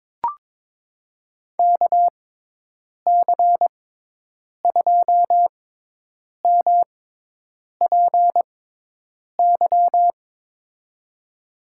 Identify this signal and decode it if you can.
Morse code, heard as a keyed tone, about 22 wpm, spaced as for 8 wpm, 700 Hz. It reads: KC2MPY
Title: USA Call Signs - 6 Characters - 6x Character Spacing - 22wpm